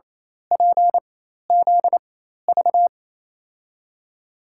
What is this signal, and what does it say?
Morse code, heard as a keyed tone, about 28 wpm, spaced as for 13 wpm, 700 Hz. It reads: PZV
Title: USA Call Signs - 6 Characters - 4x Character Spacing - 28wpm